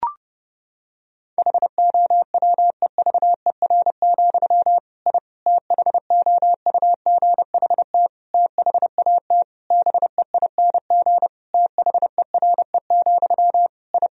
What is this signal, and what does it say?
Morse code, heard as a keyed tone, about 30 wpm, 700 Hz. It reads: HOWEVER, I THOUGHT THAT BEING THERE, I
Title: Sentences from Top 1000 Words - Review Part 3 - 30wpm